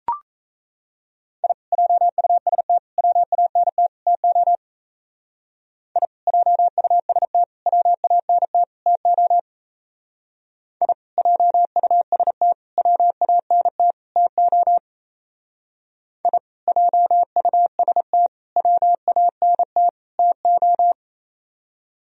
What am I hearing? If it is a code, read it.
Morse code, heard as a keyed tone, about 36 wpm, 700 Hz. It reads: I JUST WANT TO I JUST WANT TO I JUST WANT TO I JUST WANT TO